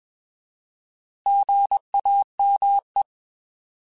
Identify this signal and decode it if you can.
Morse code, heard as a keyed tone, about 21 wpm, 800 Hz. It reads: GAME
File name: Alphabetical / Alphabetical mp3s / Gs @20